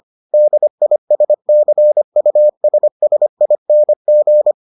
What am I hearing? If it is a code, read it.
Morse code, heard as a keyed tone, about 25 wpm, 600 Hz. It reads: DISCUSSING